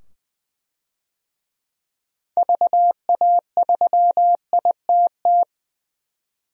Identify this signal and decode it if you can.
Morse code, heard as a keyed tone, about 20 wpm, 700 Hz. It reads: VA3ITT
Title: DXCC-100 Call Signs - 6 Characters - 20wpm